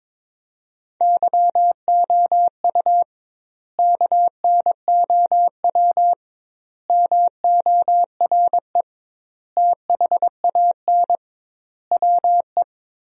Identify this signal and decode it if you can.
Morse code, heard as a keyed tone, about 21 wpm, 700 Hz. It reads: YOU KNOW MORE THAN WE
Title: 2x Word-Spacing - Sentences from Top 100 Words - 22wpm